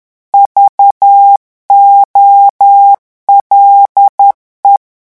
Morse code, heard as a keyed tone, about 10 wpm, 800 Hz. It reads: VOLE
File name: morse mp3 / revision francais 10 mots minute